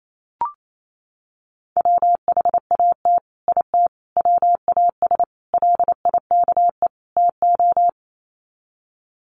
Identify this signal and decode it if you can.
Morse code, heard as a keyed tone, about 28 wpm, 700 Hz. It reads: WHAT IT WAS LIKE TO
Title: Sets of 5 Words Encore 28wpm